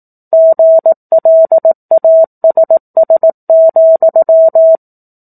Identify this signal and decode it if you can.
Morse code, heard as a keyed tone, about 18 wpm, 650 Hz. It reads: GLASS,
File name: 04-Chapter-04 18 WPM